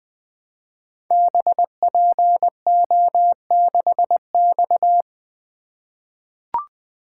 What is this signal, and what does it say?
Morse code, heard as a keyed tone, about 20 wpm, 700 Hz. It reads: BPO6X
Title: Code Groups - 5 Letter:Numbers - 20wpm